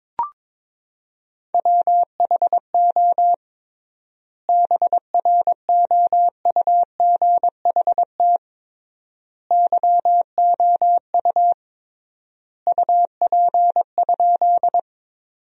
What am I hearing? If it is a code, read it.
Morse code, heard as a keyed tone, about 21 wpm, 700 Hz. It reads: WHO BROUGHT YOU UP?
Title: 3x Word-Spacing - Sentences from Top 500 Words - 22wpm